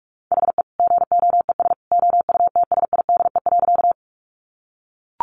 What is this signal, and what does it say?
Morse code, heard as a keyed tone, about 44 wpm, 700 Hz. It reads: HE GOES OUTSIDE.
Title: Additional Easy Phrases 45wpm